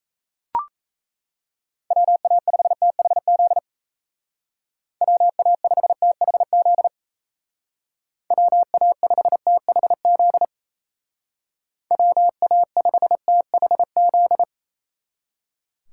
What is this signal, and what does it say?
Morse code, heard as a keyed tone, about 36 wpm, 700 Hz. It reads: WA5THZ WA5THZ WA5THZ WA5THZ